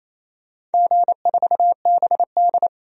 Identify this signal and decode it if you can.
Morse code, heard as a keyed tone, about 27 wpm, 700 Hz. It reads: G4BD
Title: DXCC-100 Call Signs - 4 Characters - 28wpm